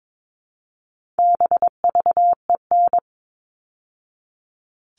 Morse code, heard as a keyed tone, about 22 wpm, 700 Hz. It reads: BVEN
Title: All Cut Numbers - 4-Digit 22wpm